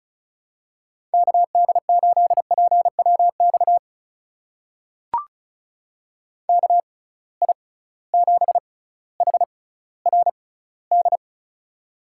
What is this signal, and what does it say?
Morse code, heard as a keyed tone, about 36 wpm, spaced as for 18 wpm, 700 Hz. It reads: KD8PWX KI7HRD
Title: USA Call Signs - 6 Characters - 6x Character Spacing - 35wpm